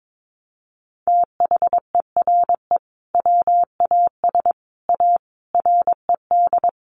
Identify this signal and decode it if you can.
Morse code, heard as a keyed tone, about 22 wpm, 700 Hz. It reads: THERE WAS A RED
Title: Sentences from Top 700 Words 22wpm